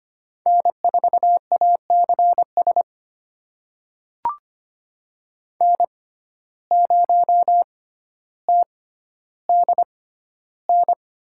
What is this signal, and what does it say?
Morse code, heard as a keyed tone, about 25 wpm, spaced as for 13 wpm, 700 Hz. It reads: N4ACS N0TDN
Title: USA Call Signs - 6x Character Spacing - 25wpm